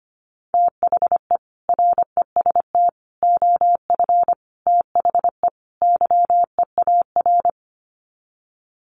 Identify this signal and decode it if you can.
Morse code, heard as a keyed tone, about 25 wpm, 690 Hz. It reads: THE REST OF THE YEAR
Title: Sets of 5 Words Encore 25wpm